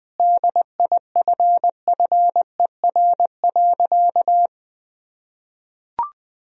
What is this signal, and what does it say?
Morse code, heard as a keyed tone, about 21 wpm, 700 Hz. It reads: DIFFER.
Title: Sentences from Top 1000 Words - Review Part 1 - 20wpm